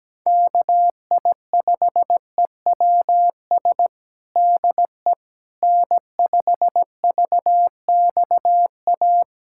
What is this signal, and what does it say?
Morse code, heard as a keyed tone, about 17 wpm, 700 Hz. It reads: KI5EWS DE N5VXA
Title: USA Callsign DE Callsign 17wpm